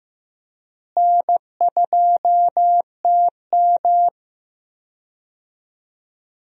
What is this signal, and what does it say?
Morse code, heard as a keyed tone, about 15 wpm, 700 Hz. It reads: N2TM
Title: USA Call Signs - 4 Characters - Repeated 3x - 15wpm